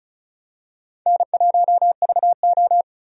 Morse code, heard as a keyed tone, about 36 wpm, 700 Hz. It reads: N1VO
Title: USA Call Signs - 4 Characters - Repeated 2x - 35wpm